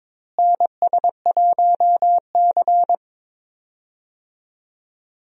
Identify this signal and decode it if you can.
Morse code, heard as a keyed tone, about 22 wpm, 700 Hz. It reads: NS1C
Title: USA Call Signs - 4 Characters - Repeated 3x - 22wpm